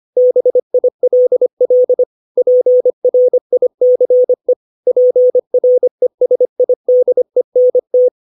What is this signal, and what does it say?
Morse code, heard as a keyed tone, about 25 wpm, 500 Hz. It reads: BILL PRICE PRESIDENT
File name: 632 words / 632 Words 25WPM